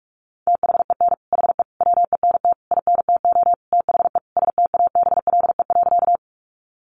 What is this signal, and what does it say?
Morse code, heard as a keyed tone, about 44 wpm, 700 Hz. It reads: THEN HE WENT INTO THE STABLE.
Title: Sentences from Top 1500 Words 45wpm